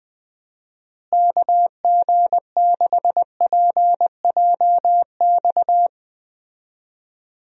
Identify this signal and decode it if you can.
Morse code, heard as a keyed tone, about 20 wpm, 700 Hz. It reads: KG6PJX
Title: USA Call Signs - 6 Characters - Repeated 2x - 20wpm